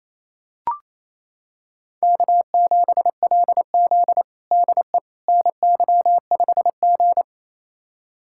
Morse code, heard as a keyed tone, about 27 wpm, 700 Hz. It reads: K7LZ DE NY5G
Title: USA Callsign DE Callsign - 4 Characters - 28wpm